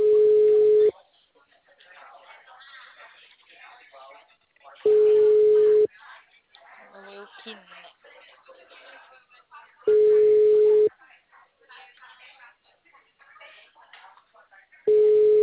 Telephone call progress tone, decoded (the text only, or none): ringback tone